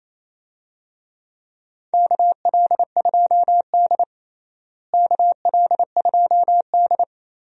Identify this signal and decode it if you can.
Morse code, heard as a keyed tone, about 27 wpm, 700 Hz. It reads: KL2D KL2D
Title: USA Call Signs - 4 Characters - Repeated 2x - 28wpm